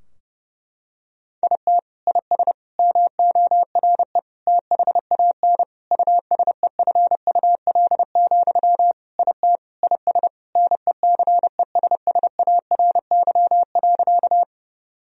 Morse code, heard as a keyed tone, about 30 wpm, 700 Hz. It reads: IT IS MORE THAN USEFUL, IT IS NECESSARY.
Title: Sentences from Top 1000 Words - Review Part 4 - 30wpm